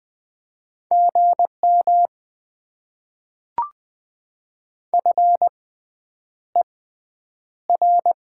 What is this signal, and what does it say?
Morse code, heard as a keyed tone, about 20 wpm, spaced as for 9 wpm, 700 Hz. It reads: GM FER
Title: Top 90 QSO Elements - 6x Character Spacing - 20wpm